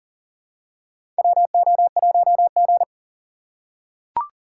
Morse code, heard as a keyed tone, about 40 wpm, 700 Hz. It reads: WO1G
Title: USA Call Signs - 4 Characters - 2x Character Spacing - 40wpm